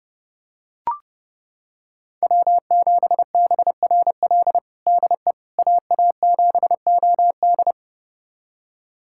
Morse code, heard as a keyed tone, about 32 wpm, 700 Hz. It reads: W7BRL DE AA7OD
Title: USA Callsign DE Callsign - 5 Characters - 30wpm